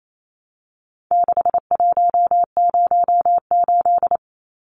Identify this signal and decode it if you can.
Morse code, heard as a keyed tone, about 28 wpm, 700 Hz. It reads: 6108
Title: Numbers - 4-Digit 28wpm